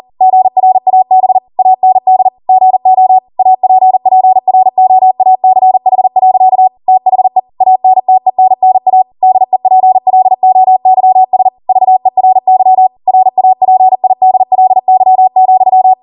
Morse code, heard as a keyed tone, about 40 wpm, 750 Hz. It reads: GRAB AND GO APPROACH. THE ANTENNA DEPLOYS VERY RAPIDLY,